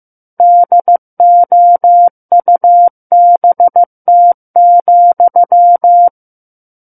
Morse code, heard as a keyed tone, about 15 wpm, 700 Hz. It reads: DOUBT,
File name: Book One / 06-Chapter 6 - 15 WPM